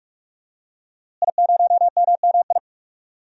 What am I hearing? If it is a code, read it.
Morse code, heard as a keyed tone, about 46 wpm, 700 Hz. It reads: I0MMI
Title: DXCC-100 Call Signs - 5 Characters - Repeated 3x - 45wpm